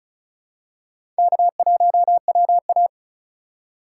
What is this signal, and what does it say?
Morse code, heard as a keyed tone, about 36 wpm, 700 Hz. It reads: K1WA
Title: USA Call Signs - 2x Character Spacing - 35wpm